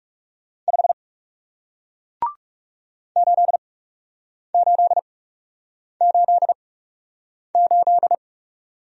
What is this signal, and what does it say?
Morse code, heard as a keyed tone, about 40 wpm, spaced as for 8 wpm, 700 Hz. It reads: 5 8888